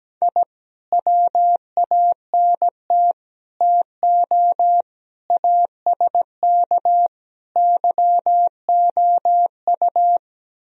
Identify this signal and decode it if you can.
Morse code, heard as a keyed tone, about 17 wpm, 700 Hz. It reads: I WANT TO ASK YOU
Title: Speed-Racing - Sets of 5 Words 17wpm